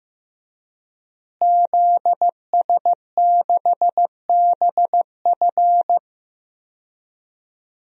Morse code, heard as a keyed tone, about 16 wpm, 700 Hz. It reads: ZS6BF
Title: DXCC-100 Call Signs - 5 Characters - Repeated 2x - 15wpm